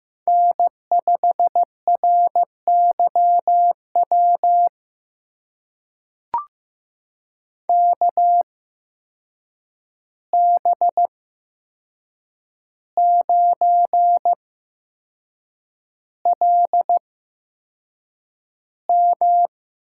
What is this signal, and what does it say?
Morse code, heard as a keyed tone, about 15 wpm, spaced as for 6 wpm, 700 Hz. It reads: N5RYW KB9LM
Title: USA Call Signs - 5 Characters - 8x Character Spacing - 15wpm